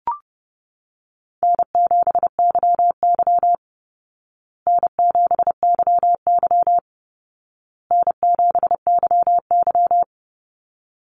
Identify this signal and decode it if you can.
Morse code, heard as a keyed tone, about 30 wpm, 700 Hz. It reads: N7YY N7YY N7YY
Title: Easy Call Signs Repeated 3x - 30wpm